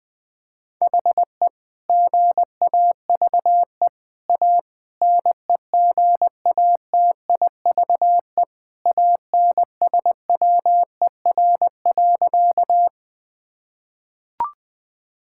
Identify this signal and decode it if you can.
Morse code, heard as a keyed tone, about 20 wpm, 700 Hz. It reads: HE GAVE A NEGATIVE ANSWER.
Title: Sentences from Top 1000 Words - Review Part 6 - 20wpm